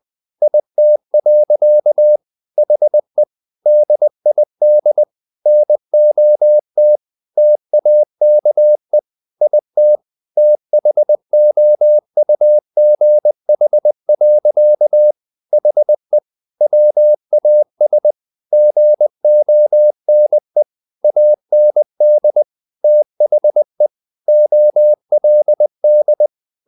Morse code, heard as a keyed tone, about 20 wpm, 600 Hz. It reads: IT. HE DID NOT TAKE IT THOUGH. HE WAS GONE AND THE OLD
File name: the old man and the sea / the old man and the sea part 1 20wpm